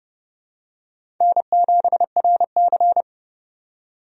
Morse code, heard as a keyed tone, about 31 wpm, 700 Hz. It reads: N7RC
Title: USA Call Signs - 4 Characters - 30wpm